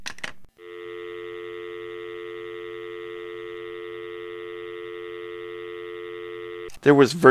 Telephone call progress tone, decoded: dial tone